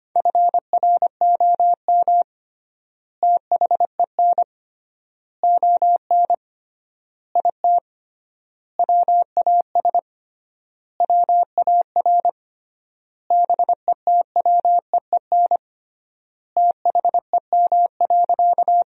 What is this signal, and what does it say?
Morse code, heard as a keyed tone, about 25 wpm, 700 Hz. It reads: FROM THEN ON IT WAS WAR BETWEEN THEM.
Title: 3x Word-Spacing - Sentences from Top 400 Words - 25wpm